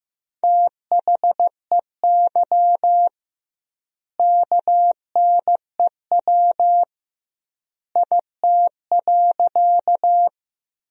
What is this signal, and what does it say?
Morse code, heard as a keyed tone, about 15 wpm, 700 Hz. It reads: THEY KNEW IT.
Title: 2x Word-Spacing - Sentences from Top 400 Words - 15wpm